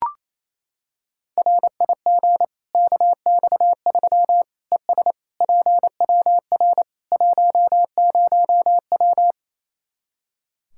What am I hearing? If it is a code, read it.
Morse code, heard as a keyed tone, about 28 wpm, 700 Hz. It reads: RIG KX3 ES PWR 10W
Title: Practice QSO - Exchange 2 - Info - 28wpm